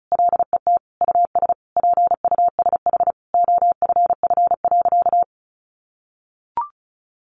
Morse code, heard as a keyed tone, about 34 wpm, 700 Hz. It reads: LET US PUSH OFF.
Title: Sentences from Top 1300 Words 35wpm